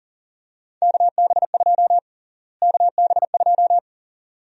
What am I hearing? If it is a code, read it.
Morse code, heard as a keyed tone, about 38 wpm, 700 Hz. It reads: KB2 KB2
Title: USA Call Sign Prefixes - Repeated 2x - 40wpm